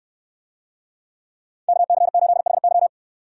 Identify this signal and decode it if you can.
Morse code, heard as a keyed tone, about 69 wpm, 690 Hz. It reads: KC8HY